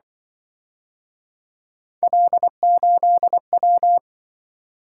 Morse code, heard as a keyed tone, about 24 wpm, 700 Hz. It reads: L8W
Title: Speed-Racing - Three Letter-Number Combinations - 20wpm